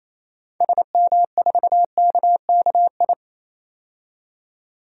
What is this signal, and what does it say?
Morse code, heard as a keyed tone, about 28 wpm, 700 Hz. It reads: SM4KKI
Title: DXCC-100 Call Signs - 6 Characters - 28wpm